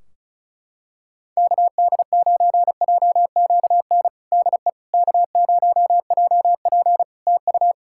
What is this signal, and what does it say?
Morse code, heard as a keyed tone, about 36 wpm, 700 Hz. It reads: KD9JQN DE K0JP TU
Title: Practice QSO - Exchange 4 - Ending - 35wpm